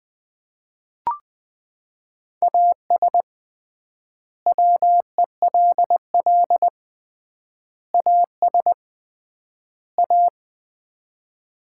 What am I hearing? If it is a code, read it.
Morse code, heard as a keyed tone, about 20 wpm, 700 Hz. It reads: AS WELL AS A
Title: 3x Word-Spacing - Sets of 4 Words - 20wpm